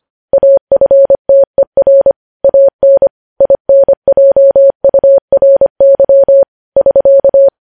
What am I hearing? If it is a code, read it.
Morse code, heard as a keyed tone, about 25 wpm, 550 Hz. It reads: AFTER AN INJURY <SK>
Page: News Headlines in Morse Code at 25 WPM